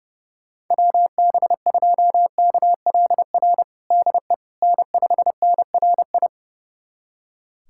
Morse code, heard as a keyed tone, about 32 wpm, 700 Hz. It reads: WB2KLR DE N5NRI
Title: USA Callsign DE Callsign 30wpm